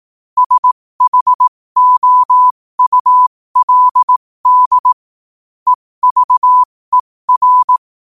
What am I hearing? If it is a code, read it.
Morse code, heard as a keyed tone, about 18 wpm, spaced as for 15 wpm, 1000 Hz. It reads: SHOULD EVER
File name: qotdfiles / 15wpm